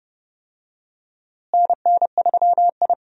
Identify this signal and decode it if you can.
Morse code, heard as a keyed tone, about 30 wpm, 700 Hz. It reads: NN3I
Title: USA Call Signs - 4 Characters - Repeated 3x - 30wpm